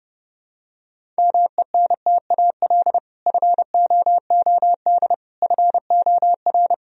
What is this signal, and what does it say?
Morse code, heard as a keyed tone, about 30 wpm, 700 Hz. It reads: MENTAL FOOD FOR
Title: Sentences from Top 1000 Words - Review Part 4 - 30wpm